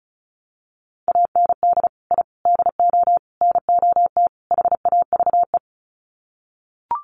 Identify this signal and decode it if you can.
Morse code, heard as a keyed tone, about 34 wpm, 700 Hz. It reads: AND I DO NOT HAVE
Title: Sets of 5 Words Encore 35wpm